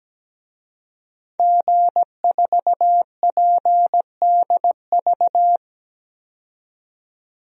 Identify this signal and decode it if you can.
Morse code, heard as a keyed tone, about 17 wpm, 700 Hz. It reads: G4PDV
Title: DXCC-100 Call Signs - 5 Characters - 17wpm